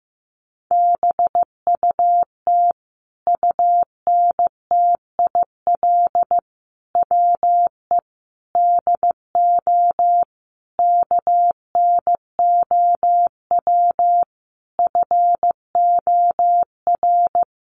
Morse code, heard as a keyed tone, about 15 wpm, 700 Hz. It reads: BUT UNTIL WE DO KNOW FOR
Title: A Sound Of Thunder 15wpm